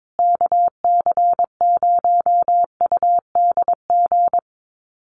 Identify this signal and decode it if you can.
Morse code, heard as a keyed tone, about 22 wpm, 700 Hz. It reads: KC0UDG
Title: USA Call Signs Repeated 3x 22wpm